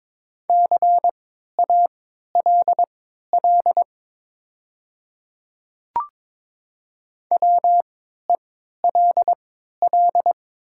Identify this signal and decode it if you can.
Morse code, heard as a keyed tone, about 22 wpm, spaced as for 13 wpm, 700 Hz. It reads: CALL WELL
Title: 3x Character Spacing - Top 200 Words - 22wpm